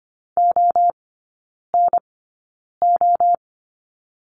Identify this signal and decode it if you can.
Morse code, heard as a keyed tone, about 25 wpm, spaced as for 9 wpm, 700 Hz. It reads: ONO
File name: Lesson 14e - Number 2 - Three Characters 25wpm